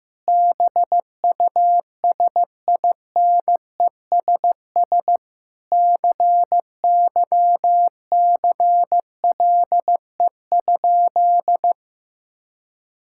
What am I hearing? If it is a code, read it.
Morse code, heard as a keyed tone, about 15 wpm, 700 Hz. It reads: BUSINESS CYCLE?
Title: Sentences from Top 1000 Words - Review Part 6 - 15wpm